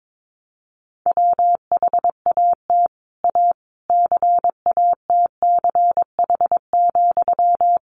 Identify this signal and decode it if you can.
Morse code, heard as a keyed tone, about 22 wpm, 700 Hz. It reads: WHAT A CATCH,
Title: Sentences from Top 1500 Words 22wpm